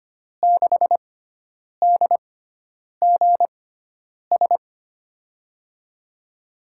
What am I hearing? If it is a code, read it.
Morse code, heard as a keyed tone, about 25 wpm, spaced as for 9 wpm, 700 Hz. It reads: 6DGS
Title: USA Call Signs - 5 Characters - 6x Character Spacing - 25wpm